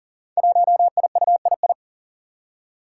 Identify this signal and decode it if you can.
Morse code, heard as a keyed tone, about 42 wpm, 700 Hz. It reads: 1IUII
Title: Five Challenging Letter-Number Combinations 40wpm